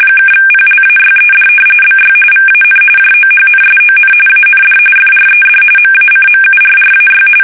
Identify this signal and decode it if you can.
RTTY, about 45.45 baud. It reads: (PLAYED BY VAN JOHNSON AND FRED MACMURRAY
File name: fsk850 / battleshipnj